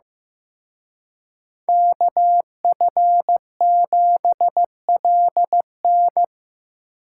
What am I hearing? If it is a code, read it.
Morse code, heard as a keyed tone, about 18 wpm, 700 Hz. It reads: KF7LN
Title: Speed Racing - USA 2x2 Call Signs 15wpm